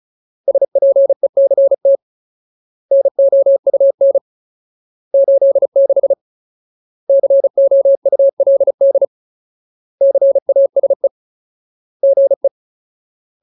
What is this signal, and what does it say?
Morse code, heard as a keyed tone, about 37 wpm, 550 Hz. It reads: SPECT NOUN 86 COULD CASE GE
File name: Ultimate Word List / 35WPM Ultimate